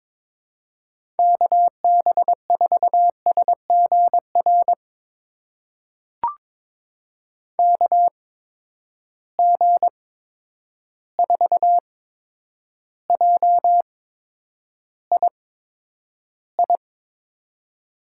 Morse code, heard as a keyed tone, about 22 wpm, spaced as for 9 wpm, 700 Hz. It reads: KB4SGR KG4JII
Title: USA Call Signs - 6 Characters - 8x Character Spacing - 22wpm